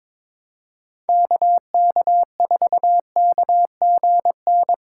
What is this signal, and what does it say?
Morse code, heard as a keyed tone, about 22 wpm, 700 Hz. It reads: KK4KGN
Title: USA Call Signs - 6 Characters - 2x Character Spacing - 22wpm